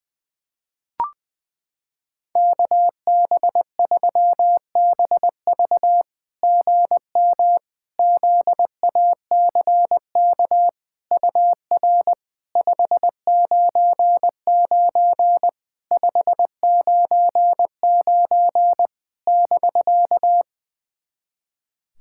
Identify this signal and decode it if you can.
Morse code, heard as a keyed tone, about 21 wpm, 700 Hz. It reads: KB3BV GM ZACK UR 599 599 <BK>
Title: SOTA QSO Practice 20wpm